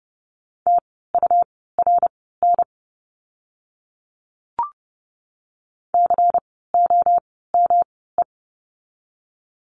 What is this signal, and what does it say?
Morse code, heard as a keyed tone, about 30 wpm, spaced as for 17 wpm, 700 Hz. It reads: TURN COME
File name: course / 30:10wpm / 230 Lesson 37f - Letter x - Words 30wpm